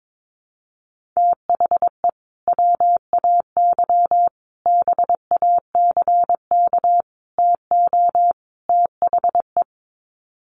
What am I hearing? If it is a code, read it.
Morse code, heard as a keyed tone, about 22 wpm, 700 Hz. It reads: THE WAY BACK TO THE
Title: Sets of 5 Words Encore2 22wpm